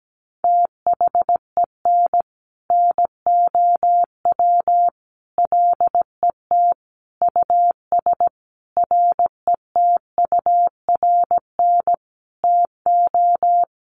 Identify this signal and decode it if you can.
Morse code, heard as a keyed tone, about 17 wpm, 700 Hz. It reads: THEN NOW LET US RETURN TO
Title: Sentences from Top 1500 Words 17wpm